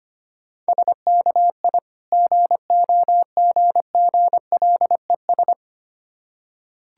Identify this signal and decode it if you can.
Morse code, heard as a keyed tone, about 25 wpm, 700 Hz. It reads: SKI GOGGLES
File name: practice / 25wpm / Difficult Words with Hints - 25wpm